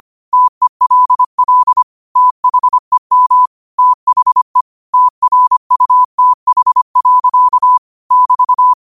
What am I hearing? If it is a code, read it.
Morse code, heard as a keyed tone, about 25 wpm, 1000 Hz. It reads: TELL THEM THE TRUTH. <BT>
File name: qotdfiles / 25wpm